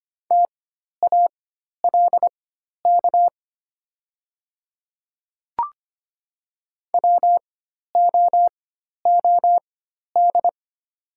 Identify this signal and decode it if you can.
Morse code, heard as a keyed tone, about 25 wpm, spaced as for 12 wpm, 700 Hz. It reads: TALK WOOD
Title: 4x Character Spacing - Top 500 Words - 25wpm